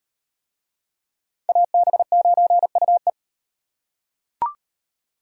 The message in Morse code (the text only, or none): AB9UE